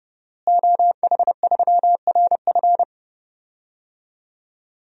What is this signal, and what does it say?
Morse code, heard as a keyed tone, about 32 wpm, 700 Hz. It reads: OH3RF
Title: DXCC-100 Call Signs - 5 Characters - Repeated 2x - 30wpm